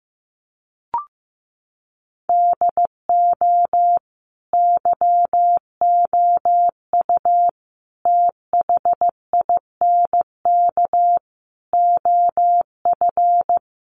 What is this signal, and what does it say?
Morse code, heard as a keyed tone, about 15 wpm, 700 Hz. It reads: DO YOU THINK OF
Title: Sets of 5 Words Encore2 15wpm